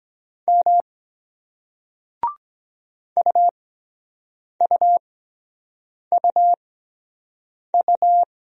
Morse code, heard as a keyed tone, about 23 wpm, spaced as for 7 wpm, 700 Hz. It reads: M UUUU